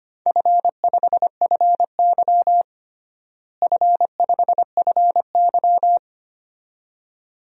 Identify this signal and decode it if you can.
Morse code, heard as a keyed tone, about 25 wpm, 700 Hz. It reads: F5FY F5FY
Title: DXCC-100 Call Signs - 4 Characters - Repeated 2x - 25wpm